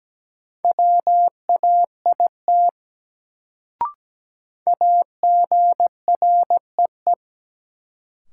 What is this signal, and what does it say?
Morse code, heard as a keyed tone, about 17 wpm, 700 Hz. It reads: WAIT AGREE
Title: Sets of 5 Words Encore - New Word Drill - 17wpm